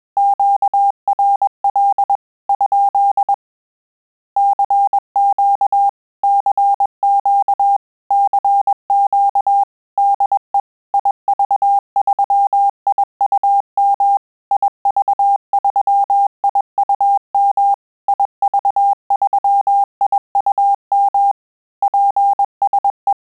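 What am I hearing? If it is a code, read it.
Morse code, heard as a keyed tone, about 21 wpm, 800 Hz. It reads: QRL? CQ CQ CQ DE IV3IUM IV3IUM IV3IUM PSE
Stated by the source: Morse QSO practice at 20 words per minute, in Italian and English.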